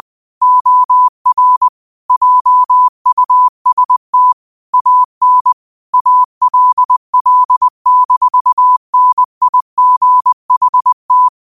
Morse code, heard as a keyed tone, about 20 wpm, 1000 Hz. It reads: OR JUST AN ALL-NIGHT
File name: qotdfiles / 20wpm